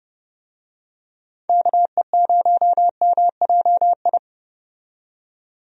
Morse code, heard as a keyed tone, about 29 wpm, 700 Hz. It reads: KE0MJI
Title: USA Call Signs - 6 Characters - 30wpm